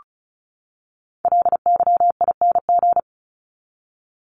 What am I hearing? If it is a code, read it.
Morse code, heard as a keyed tone, about 34 wpm, 700 Hz. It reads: LYING
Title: ING Suffix - Top 100 Words with Base Words 35wpm